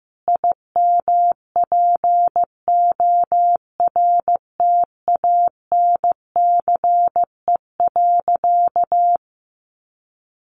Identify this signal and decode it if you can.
Morse code, heard as a keyed tone, about 14 wpm, 700 Hz. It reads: IMPORTANCE.
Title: Sentences from Top 1200 Words 15wpm